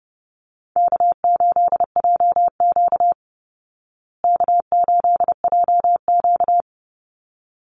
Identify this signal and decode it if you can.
Morse code, heard as a keyed tone, about 30 wpm, 700 Hz. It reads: K8JQ K8JQ
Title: Easy Call Signs Repeated 2x - 30wpm